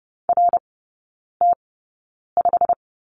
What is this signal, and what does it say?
Morse code, heard as a keyed tone, about 30 wpm, spaced as for 9 wpm, 700 Hz. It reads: RT5
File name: Lesson 18e - Letter m - Three Characters 30wpm